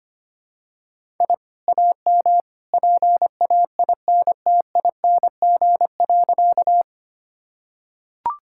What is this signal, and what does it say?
Morse code, heard as a keyed tone, about 25 wpm, 700 Hz. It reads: I AM PAINTING.
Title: Sentences from Top 1000 Words - Review Part 6 - 25wpm